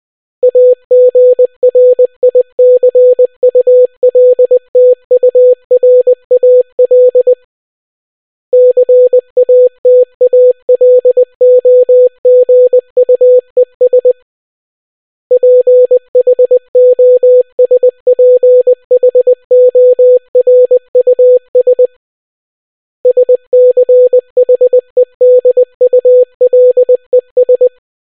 Morse code, heard as a keyed tone, about 20 wpm, 500 Hz. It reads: AGRICULTURAL CATALOGUES PHOSPHORUS SCHEDULES